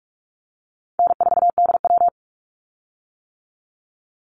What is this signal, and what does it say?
Morse code, heard as a keyed tone, about 44 wpm, 700 Hz. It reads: N4DW
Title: Call Signs - 45wpm